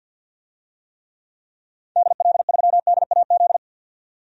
Morse code, heard as a keyed tone, about 51 wpm, 700 Hz. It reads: DL3DAZ
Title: DXCC-100 Call Signs - 6 Characters - 50wpm